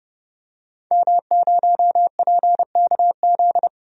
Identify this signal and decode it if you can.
Morse code, heard as a keyed tone, about 30 wpm, 700 Hz. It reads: M0PKZ
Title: DXCC-100 Call Signs - 5 Characters - 30wpm